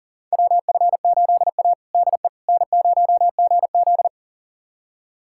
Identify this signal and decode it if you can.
Morse code, heard as a keyed tone, about 40 wpm, 700 Hz. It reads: WF8A DE N0GZ
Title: USA Callsign DE Callsign - 4 Characters - 40wpm